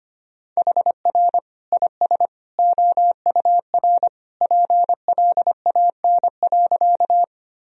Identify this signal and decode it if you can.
Morse code, heard as a keyed tone, about 25 wpm, 700 Hz. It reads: HR IS OUR PLAN.
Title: Sentences from Top 500 Words and Abbreviations 25wpm